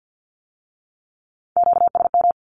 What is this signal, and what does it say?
Morse code, heard as a keyed tone, about 50 wpm, 700 Hz. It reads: QSM
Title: Q-Codes with Spoken Meaning - 50wpm